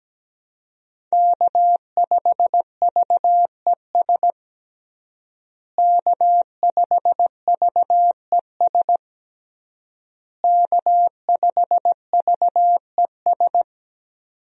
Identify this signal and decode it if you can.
Morse code, heard as a keyed tone, about 17 wpm, 700 Hz. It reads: K5VES K5VES K5VES
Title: USA Call Signs - 5 Characters - Repeated 3x - 17wpm